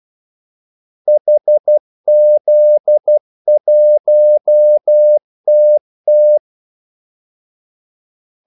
Morse code, heard as a keyed tone, about 12 wpm, 600 Hz. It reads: HZ1TT
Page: Kaldesignaler 12 wpm